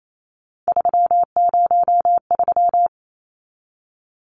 Morse code, heard as a keyed tone, about 28 wpm, 700 Hz. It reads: 303
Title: Numbers - 3-Digit 28wpm